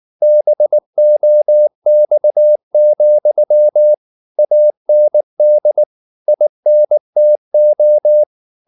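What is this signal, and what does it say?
Morse code, heard as a keyed tone, about 19 wpm, 600 Hz. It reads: BOX, AND INTO
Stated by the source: Morse code at 19 wpm from CW med Gnister, reading Ken Follett's novel Never in English.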